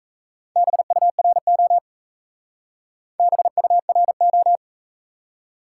Morse code, heard as a keyed tone, about 45 wpm, 700 Hz. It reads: BURO BURO